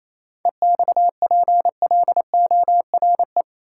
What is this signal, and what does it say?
Morse code, heard as a keyed tone, about 27 wpm, 700 Hz. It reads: EXPLORE
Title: Sentences from Top 1100 Words - New Word Drill - 28wpm